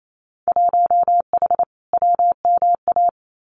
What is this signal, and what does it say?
Morse code, heard as a keyed tone, about 28 wpm, 700 Hz. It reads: 1H WMA
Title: Winter Field Day Exchanges 28wpm